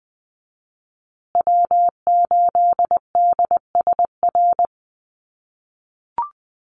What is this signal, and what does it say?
Morse code, heard as a keyed tone, about 19 wpm, 700 Hz. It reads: W8DSR
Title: USA Call Signs Repeated 3x 20wpm